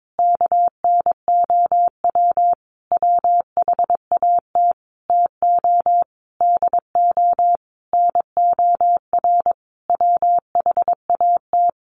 Morse code, heard as a keyed tone, about 22 wpm, 700 Hz. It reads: KNOW WHAT TO DO NOR WHAT
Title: Sentences from Top 1500 Words 22wpm